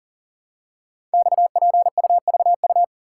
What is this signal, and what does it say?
Morse code, heard as a keyed tone, about 38 wpm, 690 Hz. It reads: XPUVU
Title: Code Groups - 5 Letters - 40wpm